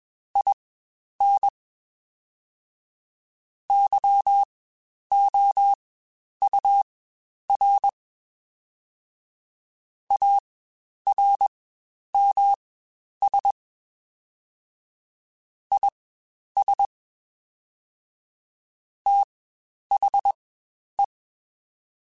Morse code, heard as a keyed tone, about 21 wpm, spaced as for 10 wpm, 800 Hz. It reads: IN YOUR ARMS IS THE
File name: Quotes Spaced / Quotes Spaced-mp3s / ChildAsleep sp @20